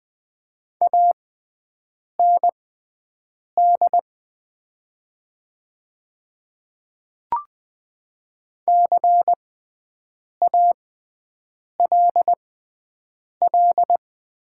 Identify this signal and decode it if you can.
Morse code, heard as a keyed tone, about 20 wpm, spaced as for 7 wpm, 700 Hz. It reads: AND CALL
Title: 6x Character Spacing - Top 100 Words - 20wpm